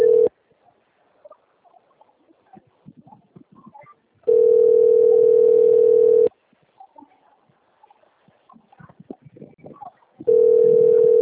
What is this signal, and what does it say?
Ringback tone (North America), 440 + 480 Hz.